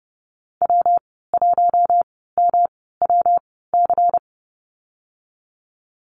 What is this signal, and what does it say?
Morse code, heard as a keyed tone, about 30 wpm, spaced as for 17 wpm, 700 Hz. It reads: W1MWC
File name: Lesson 19g - Letter w - Call Signs 30wpm